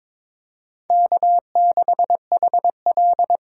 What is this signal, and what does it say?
Morse code, heard as a keyed tone, about 22 wpm, 700 Hz. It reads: K6HL
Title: USA Callsign DE Callsign - 4 Characters - 22wpm